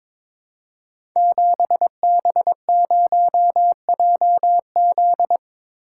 Morse code, heard as a keyed tone, about 22 wpm, 700 Hz. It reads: 7B0JZ
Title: Code Groups - 5 Letter:Numbers - 22wpm